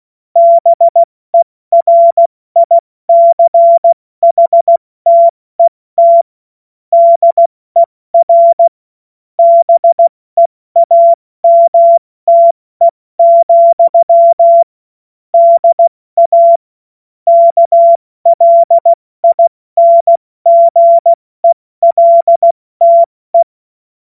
Morse code, heard as a keyed tone, about 16 wpm, 680 Hz. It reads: BERICHTET DER BEAMTE, DA KLINGELTE
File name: Witz-09-September-2025-70-F80